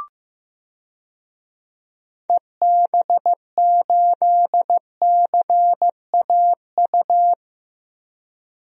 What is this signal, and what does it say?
Morse code, heard as a keyed tone, about 15 wpm, 700 Hz. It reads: EB8CAU
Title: DXCC-100 Call Signs - 6 Characters - Repeated 3x - 15wpm